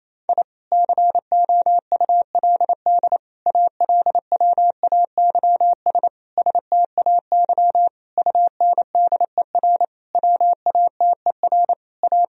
Morse code, heard as a keyed tone, about 28 wpm, 700 Hz. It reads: I COULD ALWAYS STAY UNDER WATER A
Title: Sentences from Top 1000 Words - Review Part 1 - 28wpm